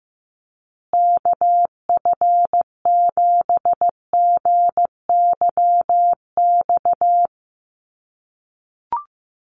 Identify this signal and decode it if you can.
Morse code, heard as a keyed tone, about 15 wpm, 700 Hz. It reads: KF7GYX
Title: USA Call Signs Repeated 3x 15wpm